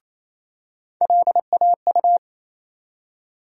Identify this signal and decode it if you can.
Morse code, heard as a keyed tone, about 28 wpm, 700 Hz. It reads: LAU
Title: USA Call Sign Postfixes - 3 Letters 28wpm